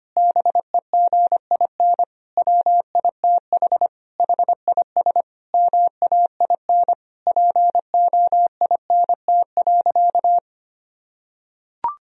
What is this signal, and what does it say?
Morse code, heard as a keyed tone, about 25 wpm, 700 Hz. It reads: BEGIN WITH HIS MAIN POINT.
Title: Sentences from Top 1000 Words - Review Part 2 - 25wpm